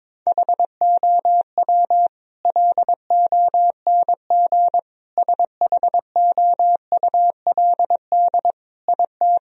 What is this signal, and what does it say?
Morse code, heard as a keyed tone, about 22 wpm, 700 Hz. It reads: HOW LONG SHOULD IT
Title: Sentences from Top 1000 Words - Review Part 4 - 22wpm